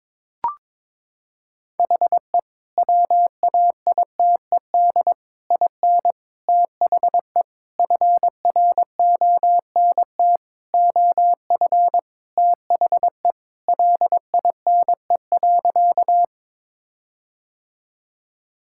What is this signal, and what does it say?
Morse code, heard as a keyed tone, about 22 wpm, 700 Hz. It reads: HE WAITED IN THE FRONT OF THE LINE.
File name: practice / 22wpm / Sentences with 4-Word Phrases - 22wpm